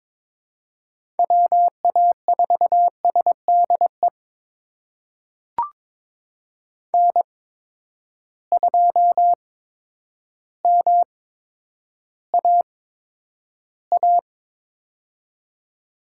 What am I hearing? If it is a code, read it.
Morse code, heard as a keyed tone, about 22 wpm, spaced as for 10 wpm, 700 Hz. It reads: WA4SDE N2MAA
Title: USA Call Signs - 8x Character Spacing - 22wpm